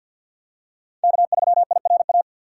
Morse code, heard as a keyed tone, about 52 wpm, 700 Hz. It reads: K3IRA
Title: USA Call Signs - 5 Characters - 3x Character Spacing - 50wpm